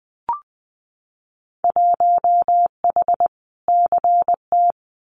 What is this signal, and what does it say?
Morse code, heard as a keyed tone, about 20 wpm, 700 Hz. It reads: 1H CT
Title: Winter Field Day Exchanges 20wpm